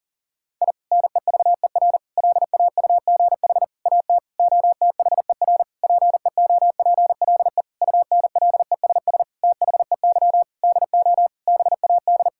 Morse code, heard as a keyed tone, about 37 wpm, 700 Hz. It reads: I NEVER LAUGH AT OTHER PEOPLE UNLESS THEY DO BAD
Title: Sentences from Top 1000 Words - Review Part 6 - 40wpm